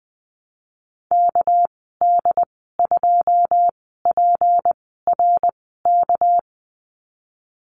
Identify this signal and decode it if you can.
Morse code, heard as a keyed tone, about 20 wpm, spaced as for 15 wpm, 700 Hz. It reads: KD2PRK
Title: 200 Lesson 32g - Letter k - Call Signs 20wpm